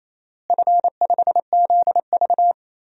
Morse code, heard as a keyed tone, about 27 wpm, 700 Hz. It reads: F5ZV
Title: DXCC-100 Call Signs - 4 Characters - Repeated 2x - 28wpm